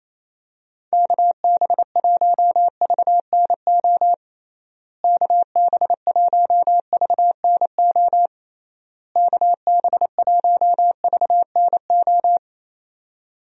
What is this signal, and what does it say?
Morse code, heard as a keyed tone, about 28 wpm, 700 Hz. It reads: KB1VNO KB1VNO KB1VNO
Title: USA Call Signs - 6 Characters - Repeated 3x - 28wpm